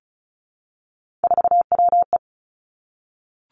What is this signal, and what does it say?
Morse code, heard as a keyed tone, about 36 wpm, 700 Hz. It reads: 4WE
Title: Three Challenging Letter-Number Combinations 35wpm